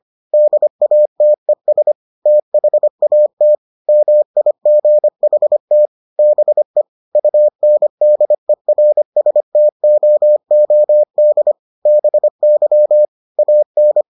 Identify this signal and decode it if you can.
Morse code, heard as a keyed tone, about 25 wpm, 600 Hz. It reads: DATES THAT MIGHT BE UNDERSTOOD BY AN